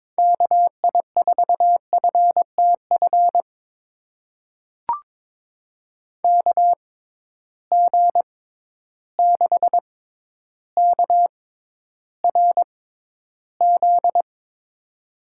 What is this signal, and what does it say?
Morse code, heard as a keyed tone, about 22 wpm, spaced as for 11 wpm, 700 Hz. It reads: KI4FTF KG6KRZ
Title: USA Call Signs - 6 Characters - 6x Character Spacing - 22wpm